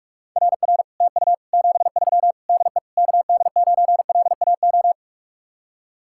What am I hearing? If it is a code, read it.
Morse code, heard as a keyed tone, about 49 wpm, 700 Hz. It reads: RR TU 73 DE KD9LAO